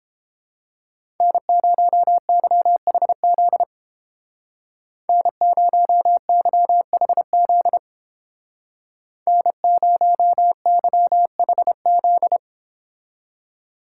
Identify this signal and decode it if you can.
Morse code, heard as a keyed tone, about 28 wpm, 700 Hz. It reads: N0YHZ N0YHZ N0YHZ